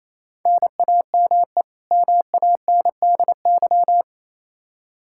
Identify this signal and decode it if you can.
Morse code, heard as a keyed tone, about 28 wpm, 700 Hz. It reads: NAME MANDY
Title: Practice QSO - Exchange 1 - Beginning - 28wpm